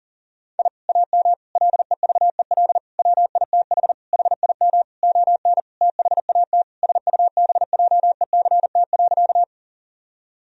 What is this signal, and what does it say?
Morse code, heard as a keyed tone, about 40 wpm, 700 Hz. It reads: I AM LEVEL WITH HIM ON THAT SUBJECT.
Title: Sentences from Top 1000 Words - Review Part 4 - 40wpm